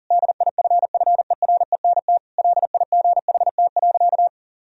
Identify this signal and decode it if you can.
Morse code, heard as a keyed tone, about 40 wpm, 700 Hz. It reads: DIFFERENT LIGHT.
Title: Sentences from Top 1000 Words - Review Part 3 - 40wpm